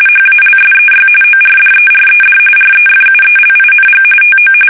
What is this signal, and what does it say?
RTTY, about 45.45 baud. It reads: HALSEY ORDERED REFUELING TO
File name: fsk850 / battleshipnj